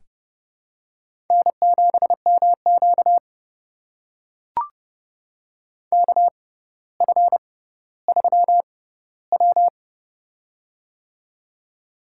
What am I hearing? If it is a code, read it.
Morse code, heard as a keyed tone, about 30 wpm, spaced as for 15 wpm, 700 Hz. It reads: N7MQ KF3W
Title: USA Call Signs - 4 Characters - 6x Character Spacing - 30wpm